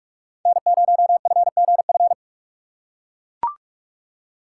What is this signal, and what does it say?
Morse code, heard as a keyed tone, about 41 wpm, 700 Hz. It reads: N0FGF